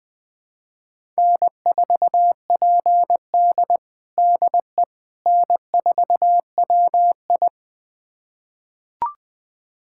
Morse code, heard as a keyed tone, about 20 wpm, 700 Hz. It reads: N4PD DE N4WI
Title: USA Callsign DE Callsign - 4 Characters - 20wpm